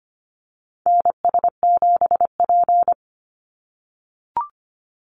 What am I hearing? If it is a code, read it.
Morse code, heard as a keyed tone, about 25 wpm, 690 Hz. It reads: NS7P
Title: NA SOTA Activator Callsigns 25wpm